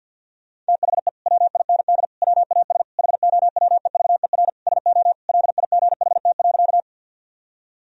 Morse code, heard as a keyed tone, about 52 wpm, 700 Hz. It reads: THE WIND WAS HOWEVER SO LIGHT.
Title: Sentences from Top 1000 Words - Review Part 2 - 50wpm